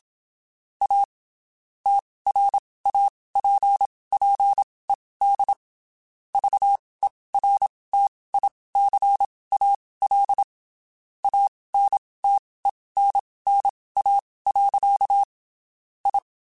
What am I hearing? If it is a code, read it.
Morse code, heard as a keyed tone, about 27 wpm, spaced as for 19 wpm, 800 Hz. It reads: A TRAPPED VERTICAL ANTENNA. I
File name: (spaced) / QSOs Spaced-mp3s / kc7haa @25 sp